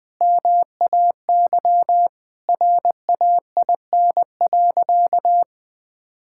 Morse code, heard as a keyed tone, about 20 wpm, 700 Hz. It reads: MAY RAIN.
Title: Sentences with 8-Word Phrases - Spoken Beginning - 20wpm